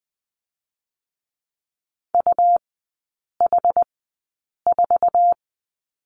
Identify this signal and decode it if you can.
Morse code, heard as a keyed tone, about 20 wpm, spaced as for 8 wpm, 700 Hz. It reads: UH4
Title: 133 Lesson 22e - Question Mark - Three Characters 20wpm